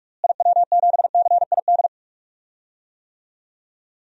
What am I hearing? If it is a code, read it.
Morse code, heard as a keyed tone, about 46 wpm, 700 Hz. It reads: IW7CID
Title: DXCC-100 Call Signs - 6 Characters - Repeated 2x - 45wpm